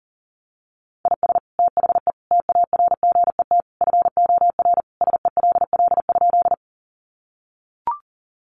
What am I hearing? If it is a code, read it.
Morse code, heard as a keyed tone, about 38 wpm, 700 Hz. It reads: IS THE TARGET FOR SELL?
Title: Sentences from Top 1200 Words 40wpm